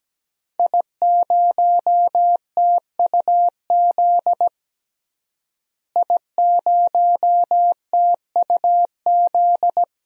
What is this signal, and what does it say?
Morse code, heard as a keyed tone, about 17 wpm, 700 Hz. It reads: I0TUZ I0TUZ
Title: DXCC-100 Call Signs - 5 Characters - Repeated 3x - 17wpm